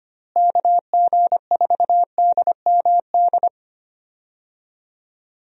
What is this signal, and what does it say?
Morse code, heard as a keyed tone, about 25 wpm, 700 Hz. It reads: KG4DMD
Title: USA Call Signs - 6 Characters - Repeated 3x - 25wpm